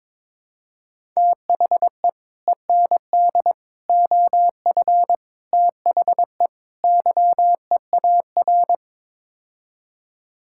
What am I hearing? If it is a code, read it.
Morse code, heard as a keyed tone, about 22 wpm, 700 Hz. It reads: THE END OF THE YEAR
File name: practice / 22wpm / Build A Phrase - 5-Word Phrases - 22wpm